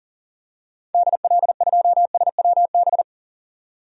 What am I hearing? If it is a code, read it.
Morse code, heard as a keyed tone, about 40 wpm, 700 Hz. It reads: DL2SWB
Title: DXCC-100 Call Signs - 6 Characters - 40wpm